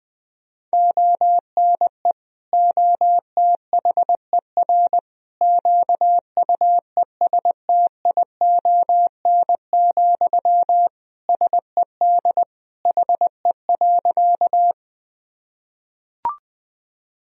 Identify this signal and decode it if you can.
Morse code, heard as a keyed tone, about 21 wpm, 700 Hz. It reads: ONE OTHER QUESTION, SED HE.
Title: Sentences from Top 400 Words and Abbreviations 20wpm